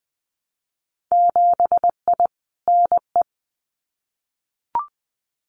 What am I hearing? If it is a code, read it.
Morse code, heard as a keyed tone, about 20 wpm, 700 Hz. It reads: 7I NE
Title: Winter Field Day Exchanges Encore 20wpm